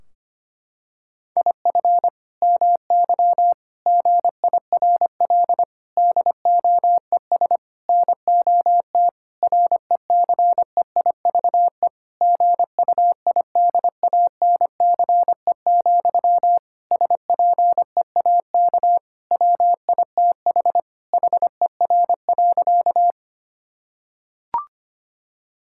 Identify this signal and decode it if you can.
Morse code, heard as a keyed tone, about 25 wpm, 700 Hz. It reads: IF MY GIRL DOES NOT RECEIVE GUIDANCE, SPEAK WITH HER.
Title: Sentences from Top 1600 Words 25wpm